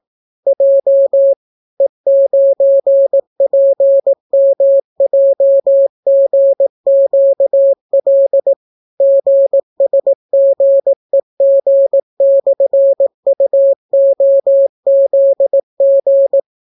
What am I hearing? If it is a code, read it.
Morse code, heard as a keyed tone, about 18 wpm, 550 Hz. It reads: J E9PMJGQL GSGEG/UOZG